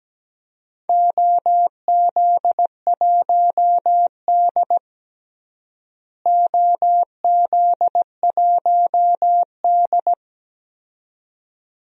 Morse code, heard as a keyed tone, about 17 wpm, 700 Hz. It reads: OZ1D OZ1D
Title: DXCC-100 Call Signs - 4 Characters - Repeated 2x - 17wpm